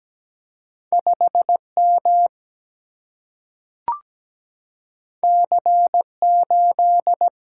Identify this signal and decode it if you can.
Morse code, heard as a keyed tone, about 17 wpm, 700 Hz. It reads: 5 M C 8
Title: Single-Letter Single-Number 17wpm